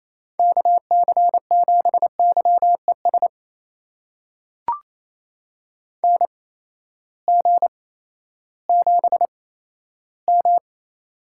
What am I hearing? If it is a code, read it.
Morse code, heard as a keyed tone, about 28 wpm, spaced as for 14 wpm, 700 Hz. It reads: KC7YES NG7M
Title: USA Call Signs - 8x Character Spacing - 28wpm